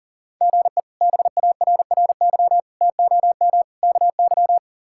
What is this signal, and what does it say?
Morse code, heard as a keyed tone, about 40 wpm, 700 Hz. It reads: GE BARRY TOM KY
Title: K1USN SST QSO Practice - 40wpm